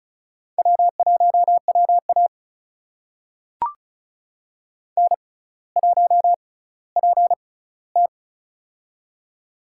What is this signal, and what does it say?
Morse code, heard as a keyed tone, about 36 wpm, spaced as for 18 wpm, 700 Hz. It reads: W1WA N1PT
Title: USA Call Signs - 4 Characters - 6x Character Spacing - 35wpm